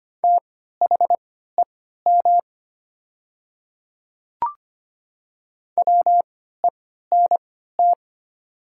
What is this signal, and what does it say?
Morse code, heard as a keyed tone, about 25 wpm, spaced as for 14 wpm, 700 Hz. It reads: THEM WENT
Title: 3x Character Spacing - Top 200 Words - 25wpm